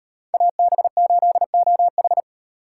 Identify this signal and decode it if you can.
Morse code, heard as a keyed tone, about 35 wpm, 700 Hz. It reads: AB8OH